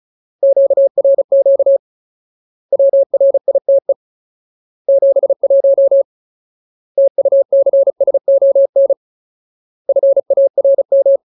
Morse code, heard as a keyed tone, about 34 wpm, 550 Hz. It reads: QRQ WRITE 71 TUCSON FARM
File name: Ultimate Word List / 35WPM Ultimate